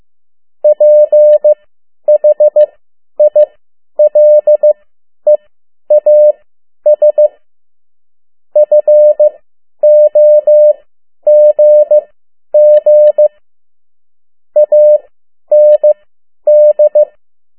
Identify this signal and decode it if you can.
Morse code, heard as a keyed tone, about 15 wpm, spaced as for 10 wpm, 600 Hz. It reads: PHILEAS FOGG AND
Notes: Around the World in 80 Days paragraph one Morse code 10 WPM